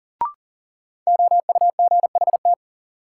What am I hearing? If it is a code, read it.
Morse code, heard as a keyed tone, about 40 wpm, 700 Hz. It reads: OUGHT
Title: Sets of 5 Words Encore - New Word Drill - 40wpm